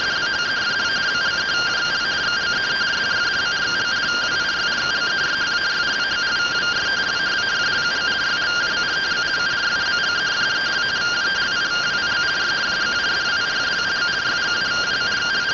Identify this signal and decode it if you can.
Two-tone teleprinter signal, about 170 Hz shift.